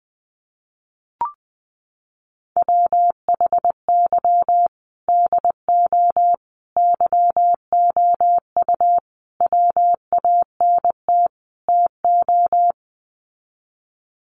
Morse code, heard as a keyed tone, about 19 wpm, 700 Hz. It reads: WHY DO YOU WANT TO
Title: Sets of 5 Words Encore2 20wpm